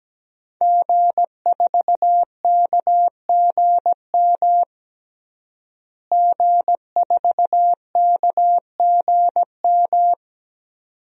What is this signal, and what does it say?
Morse code, heard as a keyed tone, about 17 wpm, 700 Hz. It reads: G4KGM G4KGM
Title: DXCC-100 Call Signs - 5 Characters - Repeated 2x - 17wpm